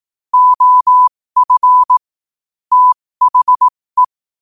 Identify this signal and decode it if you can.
Morse code, heard as a keyed tone, about 18 wpm, spaced as for 15 wpm, 1000 Hz. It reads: OF THE
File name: 15wpm